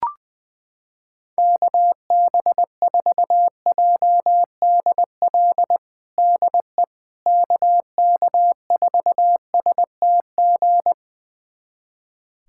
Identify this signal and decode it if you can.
Morse code, heard as a keyed tone, about 21 wpm, 700 Hz. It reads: KB4JDL DE KK4STG
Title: USA Callsign DE Callsign 20wpm